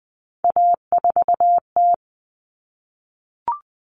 Morse code, heard as a keyed tone, about 20 wpm, 700 Hz. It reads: A4T
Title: All Cut Numbers - 3-Digit 20wpm